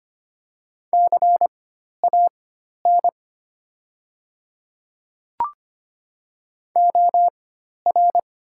Morse code, heard as a keyed tone, about 25 wpm, spaced as for 12 wpm, 700 Hz. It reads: CAN OR
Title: 4x Character Spacing - Top 200 Words - 25wpm